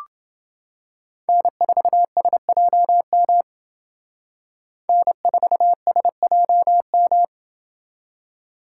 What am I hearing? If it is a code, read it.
Morse code, heard as a keyed tone, about 29 wpm, 700 Hz. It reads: N4SJM N4SJM